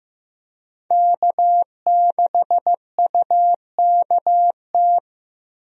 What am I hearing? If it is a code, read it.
Morse code, heard as a keyed tone, about 15 wpm, 700 Hz. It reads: K6UKT
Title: USA Call Signs - 5 Characters - Repeated 3x - 15wpm